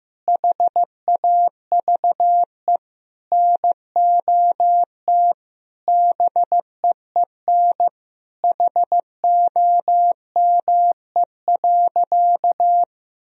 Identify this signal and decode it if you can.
Morse code, heard as a keyed tone, about 15 wpm, 700 Hz. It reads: HAVE NOT BEEN HOME.
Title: Sentences from Top 1000 Words - Review Part 1 - 15wpm